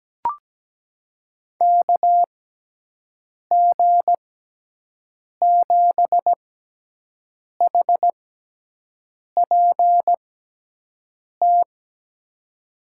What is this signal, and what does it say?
Morse code, heard as a keyed tone, about 17 wpm, spaced as for 6 wpm, 700 Hz. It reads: KG7HPT
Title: USA Call Signs - 6 Characters - 6x Character Spacing - 17wpm